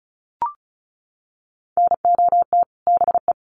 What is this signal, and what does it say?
Morse code, heard as a keyed tone, about 34 wpm, 700 Hz. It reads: NOT BE
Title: Sets of 2 Words 35wpm